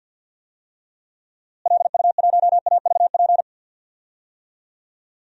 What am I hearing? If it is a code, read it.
Morse code, heard as a keyed tone, about 52 wpm, 700 Hz. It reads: LU1AVP
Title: DXCC-100 Call Signs - 6 Characters - Repeated 3x - 50wpm